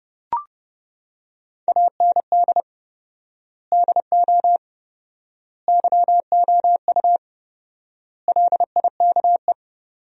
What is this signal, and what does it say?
Morse code, heard as a keyed tone, about 32 wpm, 700 Hz. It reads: AND DO YOU LIKE
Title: 4x Word-Spacing - Sentences from Top 100 Words - 30wpm